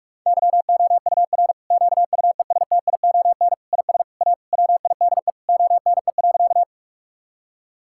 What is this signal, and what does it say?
Morse code, heard as a keyed tone, about 42 wpm, 700 Hz. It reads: YOUR QUESTION IS A WIDE ONE.
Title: Sentences from Top 1000 Words - Review Part 6 - 45wpm